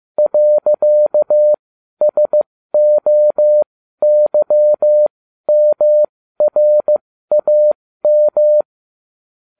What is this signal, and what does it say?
Morse code, heard as a keyed tone, about 15 wpm, spaced as for 12 wpm, 600 Hz. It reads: .SOYMRAM